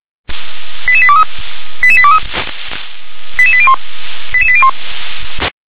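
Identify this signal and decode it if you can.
Five-tone selective call (ZVEI), heard as quick runs of five tones, tones of about 70 ms. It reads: 88823 88823 88801 88801
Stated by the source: Five-tone ZVEI selective call 88126, received on SDR.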